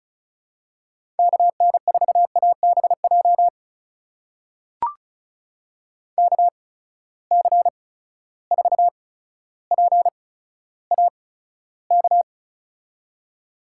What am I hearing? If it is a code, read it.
Morse code, heard as a keyed tone, about 34 wpm, spaced as for 15 wpm, 700 Hz. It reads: KN4ABJ KC4PAK
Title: USA Call Signs - 6 Characters - 8x Character Spacing - 35wpm